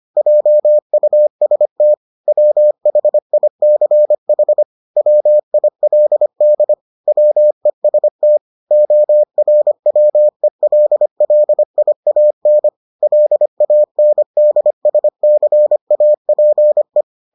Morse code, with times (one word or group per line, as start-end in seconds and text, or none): 0.2-1.9 JUST
2.3-4.6 WHICH
5.0-6.7 WILD
7.1-8.4 WEST
8.7-12.7 ORWELLIAN
13.0-17.0 LANDSCAPE